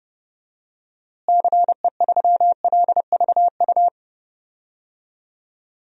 Morse code, heard as a keyed tone, about 30 wpm, 700 Hz. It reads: CE3LVU
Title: DXCC-100 Call Signs - 6 Characters - Repeated 2x - 30wpm